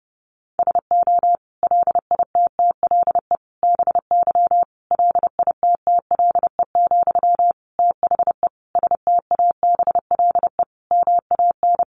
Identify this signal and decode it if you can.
Morse code, heard as a keyed tone, about 28 wpm, 700 Hz. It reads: SO LITTLE BY LITTLE, THE STABLE MAN
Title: Sentences from Top 1500 Words 30wpm